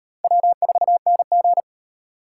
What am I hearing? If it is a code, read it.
Morse code, heard as a keyed tone, about 37 wpm, 700 Hz. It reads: W4NG